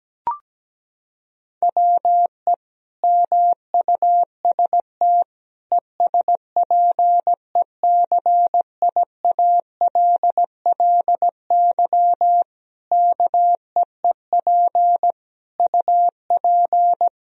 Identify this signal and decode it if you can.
Morse code, heard as a keyed tone, about 17 wpm, 700 Hz. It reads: WE MUST ESPECIALLY KEEP UP
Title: Sentences from Top 1000 Words - Review Part 4 - 17wpm